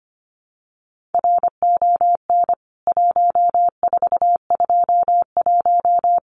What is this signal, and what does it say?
Morse code, heard as a keyed tone, about 25 wpm, 700 Hz. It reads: RON 1421
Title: CWT Exchange 25wpm